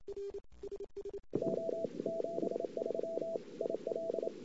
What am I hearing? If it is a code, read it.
Morse code, heard as a keyed tone, about 29 wpm, 660 Hz, with noise. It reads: O 73 IL
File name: CW / 2008-ss-cw-k5tr-op-n5rz -19:00